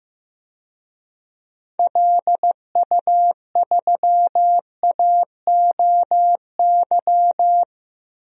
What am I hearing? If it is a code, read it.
Morse code, heard as a keyed tone, about 15 wpm, 700 Hz. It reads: LU3AOY
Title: DXCC-100 Call Signs - 6 Characters - Repeated 3x - 15wpm